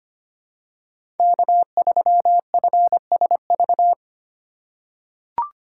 Morse code, heard as a keyed tone, about 25 wpm, 700 Hz. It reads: K3FSV
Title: USA Call Signs - 5 Characters - 4x Character Spacing - 25wpm